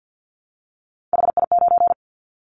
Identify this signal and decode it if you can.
Morse code, heard as a keyed tone, about 50 wpm, 700 Hz. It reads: HI9
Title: Callsign Prefixes for Top 100 DXCC - 50wpm